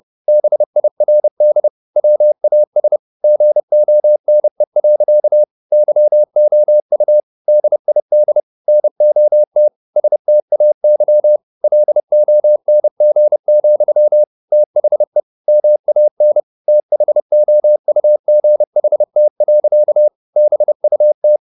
Morse code, heard as a keyed tone, about 30 wpm, 600 Hz. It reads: BIRD WAS GONE. YOU DID NOT STAY LONG, THE MAN THOUGHT. BUT
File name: the old man and the sea / the old man and the sea 1 30wpm